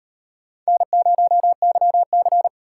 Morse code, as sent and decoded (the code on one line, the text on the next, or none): -. ----- -.-- -.-.
N0YC